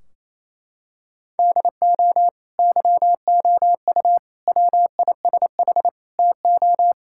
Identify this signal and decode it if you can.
Morse code, heard as a keyed tone, about 28 wpm, 700 Hz. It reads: DO YOU WISH TO
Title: Sentences from Top 1600 Words 28wpm